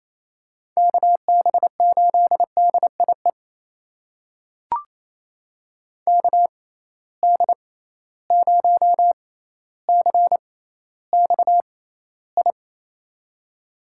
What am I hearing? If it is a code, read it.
Morse code, heard as a keyed tone, about 28 wpm, spaced as for 14 wpm, 700 Hz. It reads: KB8DIE KD0CXI
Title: USA Call Signs - 6 Characters - 6x Character Spacing - 28wpm